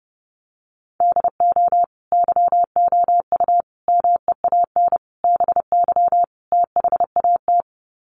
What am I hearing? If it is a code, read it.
Morse code, heard as a keyed tone, about 30 wpm, 700 Hz. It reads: DO YOU MEAN BY THAT
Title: Sets of 5 Words Encore2 30wpm